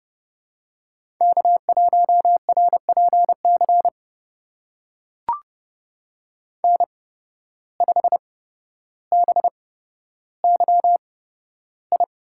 Morse code, heard as a keyed tone, about 30 wpm, spaced as for 13 wpm, 700 Hz. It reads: K1RPC N5BYI
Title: USA Call Signs - 5 Characters - 8x Character Spacing - 30wpm